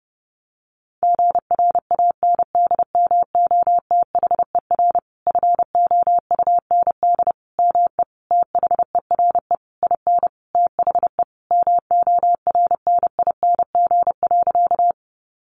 Morse code, heard as a keyed tone, about 28 wpm, 700 Hz. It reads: GRANDMOTHER FOUND ME THERE IN THE MORNING.
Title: Sentences from Top 1200 Words 30wpm